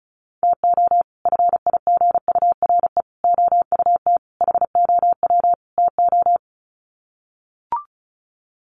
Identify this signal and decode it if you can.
Morse code, heard as a keyed tone, about 34 wpm, 700 Hz. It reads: TO FIGURE OUT HOW TO
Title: Sets of 5 Words 35wpm